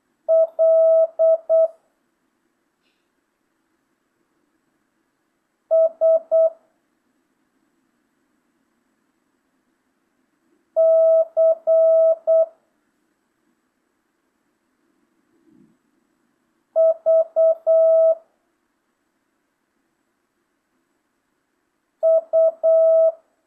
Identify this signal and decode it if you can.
Morse code, heard as a keyed tone, about 8 wpm, 650 Hz. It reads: L S C V U
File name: jh8cbh / 2025cw